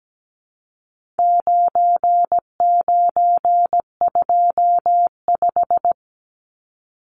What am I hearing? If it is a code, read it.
Morse code, heard as a keyed tone, about 17 wpm, 700 Hz. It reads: 9925
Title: Numbers - 4-Digit 17wpm